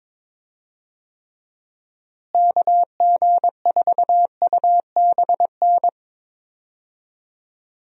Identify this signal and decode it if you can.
Morse code, heard as a keyed tone, about 22 wpm, 700 Hz. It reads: KG4UBN
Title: USA Call Signs - 6 Characters - Repeated 3x - 22wpm